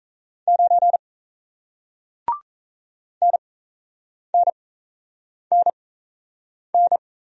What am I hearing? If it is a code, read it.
Morse code, heard as a keyed tone, about 37 wpm, spaced as for 8 wpm, 700 Hz. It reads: 9 NNNN